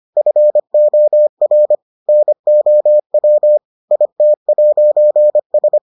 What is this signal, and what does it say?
Morse code, heard as a keyed tone, about 25 wpm, 600 Hz. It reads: FOR NOW IT'S